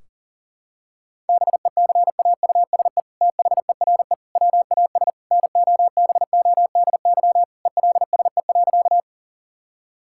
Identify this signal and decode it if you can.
Morse code, heard as a keyed tone, about 40 wpm, 700 Hz. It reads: BECAUSE THERE WAS NOBODY ELSE.
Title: Sentences from Top 1600 Words 40wpm